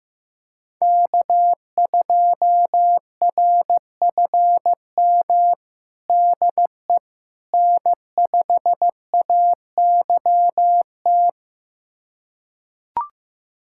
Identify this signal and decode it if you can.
Morse code, heard as a keyed tone, about 15 wpm, 700 Hz. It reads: K2RFM DE N5AYT
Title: USA Callsign DE Callsign - 5 Characters - 15wpm